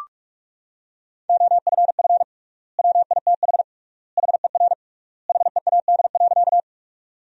Morse code, heard as a keyed tone, about 45 wpm, 700 Hz. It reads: OFF WITH HER HEAD.
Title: 3x Word-Spacing - Sentences from Top 300 Words - 45wpm